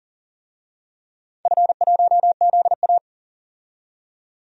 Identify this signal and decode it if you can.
Morse code, heard as a keyed tone, about 40 wpm, 700 Hz. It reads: F1ZA
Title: DXCC-100 Call Signs - 4 Characters - 40wpm